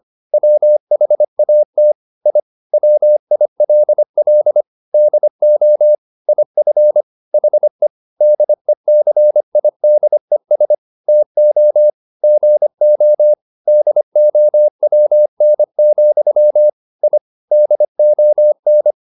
Morse code, heard as a keyed tone, about 25 wpm, 600 Hz. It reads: WHAT I WILL DO IF HE DECIDES TO GO DOWN, I DON
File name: the old man and the sea part 1 25wpm